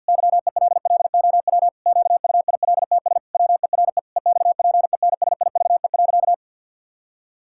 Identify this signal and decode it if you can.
Morse code, heard as a keyed tone, about 50 wpm, 700 Hz. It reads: YELLOW QUILTS WERE EXPENSIVE.
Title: Pangram Challenge v2 - 50wpm